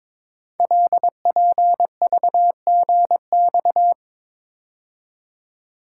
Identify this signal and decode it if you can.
Morse code, heard as a keyed tone, about 22 wpm, 700 Hz. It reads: LPVGX
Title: Code Groups - 5 Letters - 22wpm